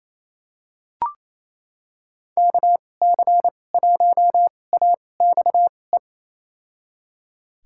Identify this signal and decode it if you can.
Morse code, heard as a keyed tone, about 28 wpm, spaced as for 20 wpm, 700 Hz. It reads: KC1AXE
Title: USA Call Signs - 6 Characters - 2x Character Spacing - 28wpm